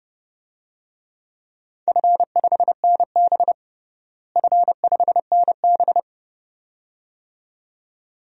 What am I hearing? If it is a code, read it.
Morse code, heard as a keyed tone, about 32 wpm, 700 Hz. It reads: F5NB F5NB
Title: DXCC-100 Call Signs - 4 Characters - Repeated 2x - 30wpm